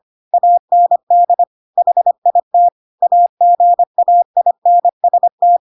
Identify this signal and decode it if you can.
Morse code, heard as a keyed tone, about 25 wpm, 700 Hz. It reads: AND HIT AGAINST
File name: Pinocchio / Pinocchio ch2 25 25 ik1hge